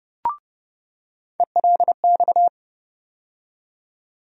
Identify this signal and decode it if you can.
Morse code, heard as a keyed tone, about 30 wpm, 700 Hz. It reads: ELX
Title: USA Call Sign Postfixes - 3 Letters 30wpm